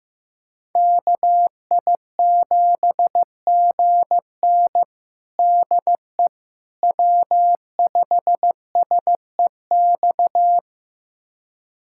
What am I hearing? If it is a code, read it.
Morse code, heard as a keyed tone, about 15 wpm, 700 Hz. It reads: KI7GN DE W5SEX
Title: USA Callsign DE Callsign - 5 Characters - 15wpm